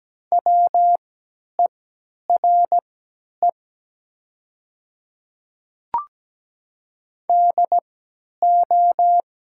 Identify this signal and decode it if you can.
Morse code, heard as a keyed tone, about 17 wpm, spaced as for 10 wpm, 700 Hz. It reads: WERE DO
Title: 3x Character Spacing - Top 200 Words - 17wpm